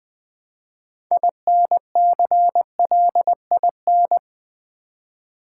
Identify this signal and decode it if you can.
Morse code, heard as a keyed tone, about 20 wpm, 700 Hz. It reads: INCLIN
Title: Build A Word - 3 Letters At A Time - 8-Letter Words - 20wpm